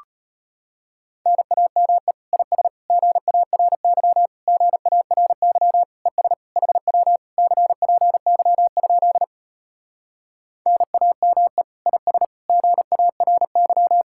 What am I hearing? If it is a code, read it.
Morse code, heard as a keyed tone, about 33 wpm, 700 Hz. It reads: NAME IS GARY GARY ES HW CPY? NAME IS GARY